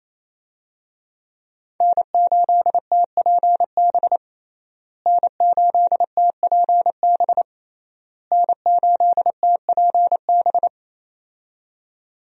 Morse code, heard as a keyed tone, about 28 wpm, 700 Hz. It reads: N8TPB N8TPB N8TPB
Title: USA Call Signs - 5 Characters - Repeated 3x - 28wpm